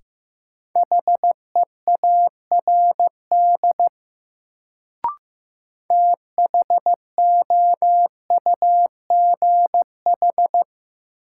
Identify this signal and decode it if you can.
Morse code, heard as a keyed tone, about 15 wpm, 700 Hz. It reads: HEARD THOUGH
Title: Sentences from Top 400 Words - New Word Drill - 15wpm